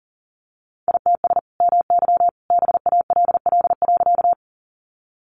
Morse code, heard as a keyed tone, about 40 wpm, 700 Hz. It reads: ITS MY BALL.
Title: Sentences from Top 900 Words 40wpm